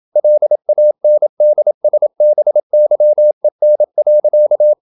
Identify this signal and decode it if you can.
Morse code, heard as a keyed tone, about 27 wpm, 600 Hz. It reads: LANDSBYEN.
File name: Ferie QRQ 27wpm